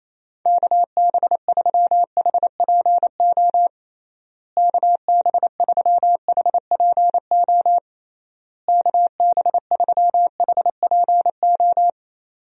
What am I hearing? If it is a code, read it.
Morse code, heard as a keyed tone, about 28 wpm, 700 Hz. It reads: KB3HPO KB3HPO KB3HPO
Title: USA Call Signs - 6 Characters - Repeated 3x - 28wpm